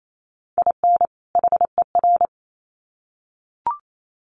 Morse code, heard as a keyed tone, about 28 wpm, 700 Hz. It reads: IN HER
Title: Sets of 2 Words 28wpm